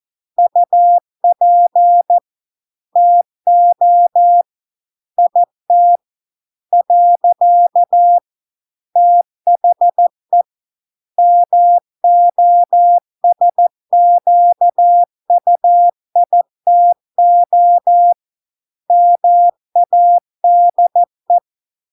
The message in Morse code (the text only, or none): UP TO IT . THE MOSQUITO MADE